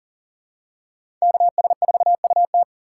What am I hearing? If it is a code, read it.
Morse code, heard as a keyed tone, about 40 wpm, 700 Hz. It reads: KS4UT
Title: USA Call Signs - 5 Characters - 4x Character Spacing - 40wpm